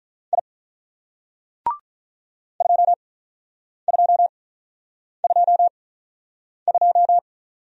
Morse code, heard as a keyed tone, about 44 wpm, spaced as for 9 wpm, 700 Hz. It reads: I 2222
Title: Speed-Racing Single Letter-Number 35wpm